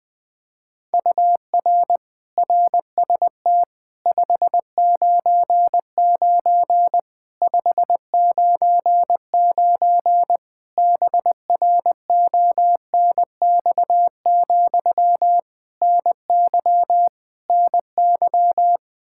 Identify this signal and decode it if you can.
Morse code, heard as a keyed tone, about 21 wpm, 700 Hz. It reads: UR RST 599 599 BRONX, NY NY
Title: Practice QSO - Exchange 1 - Beginning - 20wpm